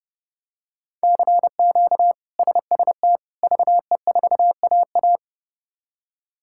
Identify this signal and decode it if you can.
Morse code, heard as a keyed tone, about 30 wpm, 700 Hz. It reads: CQ SST VE4AA
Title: K1USN SST QSO Practice - 30wpm